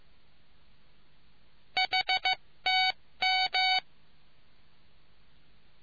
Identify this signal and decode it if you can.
Morse code, heard as a keyed tone, about 15 wpm, 750 Hz. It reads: HTM